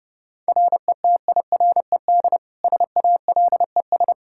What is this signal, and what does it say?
Morse code, heard as a keyed tone, about 30 wpm, 700 Hz. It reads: RETIRED SALES
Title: Practice QSO - Exchange 3 - More Info - 30wpm